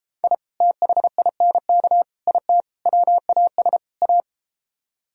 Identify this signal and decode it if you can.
Morse code, heard as a keyed tone, about 33 wpm, 700 Hz. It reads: I THINK IT WAS A